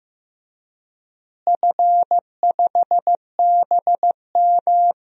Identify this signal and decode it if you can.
Morse code, heard as a keyed tone, about 15 wpm, 700 Hz. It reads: F5BM
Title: DXCC-100 Call Signs - 4 Characters - Repeated 3x - 15wpm